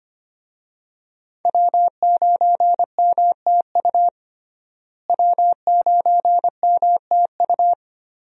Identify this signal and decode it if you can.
Morse code, heard as a keyed tone, about 25 wpm, 700 Hz. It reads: W9MTU W9MTU
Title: USA Call Signs - 5 Characters - Repeated 3x - 25wpm